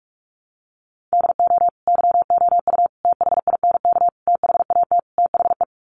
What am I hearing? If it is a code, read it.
Morse code, heard as a keyed tone, about 44 wpm, 700 Hz. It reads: DO YOU THINK THAT THE
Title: Sets of 5 Words Encore 45wpm